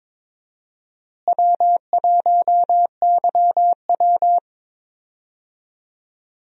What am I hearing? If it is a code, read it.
Morse code, heard as a keyed tone, about 22 wpm, 700 Hz. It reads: W1YW
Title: USA Call Signs - 4 Characters - Repeated 2x - 22wpm